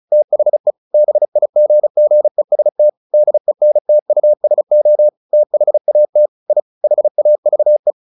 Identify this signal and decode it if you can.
Morse code, heard as a keyed tone, about 34 wpm, 600 Hz. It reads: THE BIGGEST DENTUSO THAT I HAVE
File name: the old man and the sea 2 35wpm